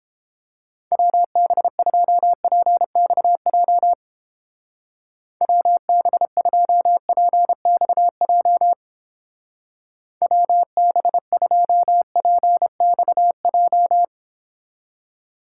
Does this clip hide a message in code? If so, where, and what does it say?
Yes: Morse code from 0.9 to 14.0 s, heard as a keyed tone, about 29 wpm, 700 Hz. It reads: WB2PXJ WB2PXJ WB2PXJ